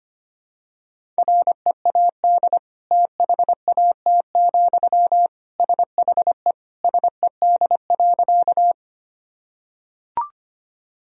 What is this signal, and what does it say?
Morse code, heard as a keyed tone, about 25 wpm, 700 Hz. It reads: READ THAT, SHE SED.
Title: Sentences from Top 200 Words and Abbreviations 25wpm